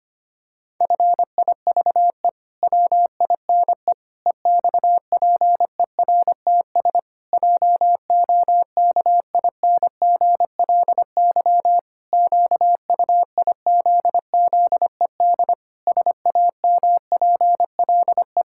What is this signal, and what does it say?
Morse code, heard as a keyed tone, about 25 wpm, 700 Hz. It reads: FIVE WINE EXPERTS JOKINGLY QUIZZED SAMPLE
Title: Pangram Challenge v2 - 25wpm